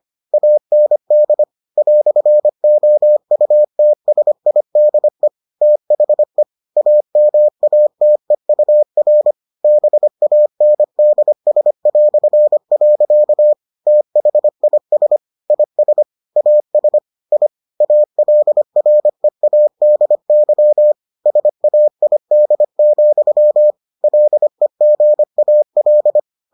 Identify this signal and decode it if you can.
Morse code, heard as a keyed tone, about 25 wpm, 600 Hz. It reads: AND "OUTSIDE THE AMATEUR BANDS". THIS IS AS I ALREADY SAID, LEGAL